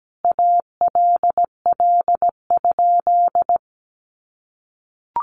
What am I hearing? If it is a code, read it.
Morse code, heard as a keyed tone, about 17 wpm, 700 Hz. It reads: ALL?
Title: Sentences from Top 800 Words 17wpm